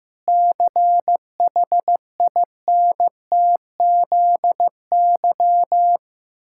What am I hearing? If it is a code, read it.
Morse code, heard as a keyed tone, about 15 wpm, 700 Hz. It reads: CHINTZY
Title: Pangram Challenge v2 - 15wpm